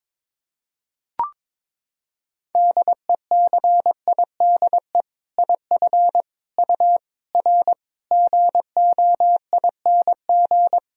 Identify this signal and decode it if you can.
Morse code, heard as a keyed tone, about 22 wpm, 700 Hz. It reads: DECIDE IF U R GOING
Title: Sentences from Top 500 Words and Abbreviations 22wpm